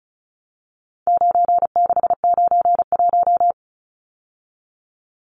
Morse code, heard as a keyed tone, about 34 wpm, 700 Hz. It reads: 9691
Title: Numbers - 4-Digit 35wpm